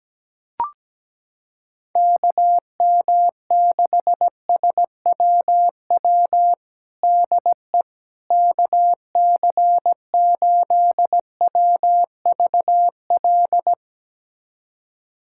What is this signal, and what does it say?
Morse code, heard as a keyed tone, about 17 wpm, 700 Hz. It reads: KM6SWW DE KC8WVL
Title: USA Callsign DE Callsign 17wpm